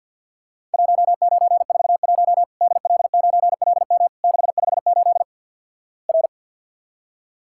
Morse code, heard as a keyed tone, about 50 wpm, 700 Hz. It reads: 1941 DL9LM 658 R
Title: Worked All Europe DX Contest - QTC Exchange - 50wpm